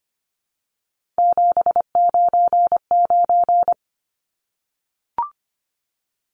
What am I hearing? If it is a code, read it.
Morse code, heard as a keyed tone, about 25 wpm, 700 Hz. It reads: 799
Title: Numbers - 3-Digit 25wpm